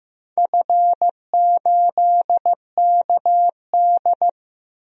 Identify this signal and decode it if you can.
Morse code, heard as a keyed tone, about 15 wpm, 700 Hz. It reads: F8KD
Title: DXCC-100 Call Signs - 4 Characters - Repeated 2x - 15wpm